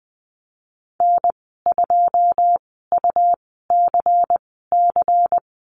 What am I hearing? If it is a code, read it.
Morse code, heard as a keyed tone, about 20 wpm, spaced as for 15 wpm, 700 Hz. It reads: N2UCC
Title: 122 Lesson 20g - Number 3 - Call Signs 20wpm